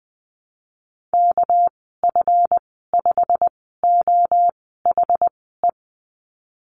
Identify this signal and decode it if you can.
Morse code, heard as a keyed tone, about 19 wpm, spaced as for 14 wpm, 700 Hz. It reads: KF5OHE
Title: 200 Lesson 32g - Letter k - Call Signs 20wpm